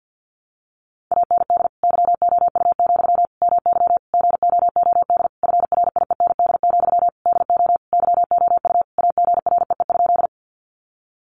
Morse code, heard as a keyed tone, about 48 wpm, 700 Hz. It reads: AND YOU, MY GOOD FRIEND, DO YOU AGREE?
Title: Sentences from Top 1100 Words 50wpm